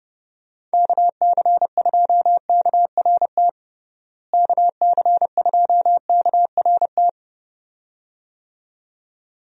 Morse code, heard as a keyed tone, about 32 wpm, 700 Hz. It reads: KC2KRT KC2KRT
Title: USA Call Signs - 6 Characters - Repeated 2x - 30wpm